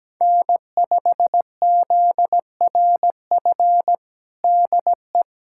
Morse code, heard as a keyed tone, about 17 wpm, 700 Hz. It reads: N5ZRF DE
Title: USA Callsign DE Callsign - 5 Characters - 17wpm